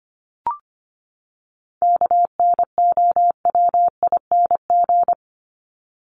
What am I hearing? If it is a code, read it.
Morse code, heard as a keyed tone, about 25 wpm, 700 Hz. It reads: KNOWING
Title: ING Suffix - Top 100 Words with Base Words 25wpm